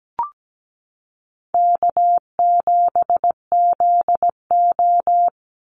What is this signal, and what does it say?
Morse code, heard as a keyed tone, about 17 wpm, 700 Hz. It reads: K7ZO
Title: NA SOTA Chaser Callsigns 17wpm